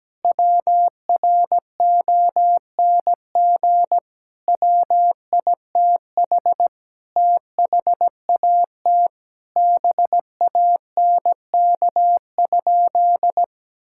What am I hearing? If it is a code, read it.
Morse code, heard as a keyed tone, about 17 wpm, 700 Hz. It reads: WRONG WITH THAT BANK?
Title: Sentences from Top 1000 Words - Review Part 6 - 17wpm